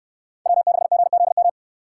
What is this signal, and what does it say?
Morse code, heard as a keyed tone, about 68 wpm, 700 Hz. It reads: W6GZM